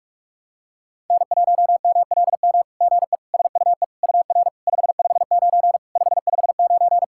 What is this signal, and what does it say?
Morse code, heard as a keyed tone, about 44 wpm, 690 Hz. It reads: N1MLM GE SUE UR 559 559
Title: SOTA QSO Practice 45wpm